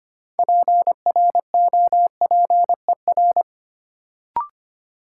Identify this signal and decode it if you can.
Morse code, heard as a keyed tone, about 25 wpm, 700 Hz. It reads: PROPER
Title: Sentences from Top 800 Words - New Word Drill - 25wpm